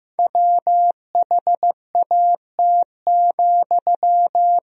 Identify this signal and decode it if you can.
Morse code, heard as a keyed tone, about 14 wpm, 700 Hz. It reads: WHAT,
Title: Sentences from Top 1000 Words - Review Part 4 - 15wpm